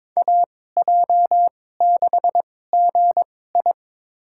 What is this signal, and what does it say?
Morse code, heard as a keyed tone, about 22 wpm, spaced as for 16 wpm, 700 Hz. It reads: AJ6GI
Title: USA Call Signs - 5 Characters - 2x Character Spacing - 22wpm